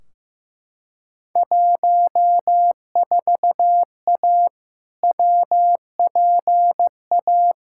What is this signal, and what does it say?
Morse code, heard as a keyed tone, about 15 wpm, 700 Hz. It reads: 14A WPA
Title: Field Day Exchanges Encore - 15wpm